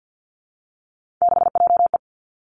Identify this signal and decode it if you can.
Morse code, heard as a keyed tone, about 50 wpm, 700 Hz. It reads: 6JE
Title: Three Challenging Letter-Number Combinations 50wpm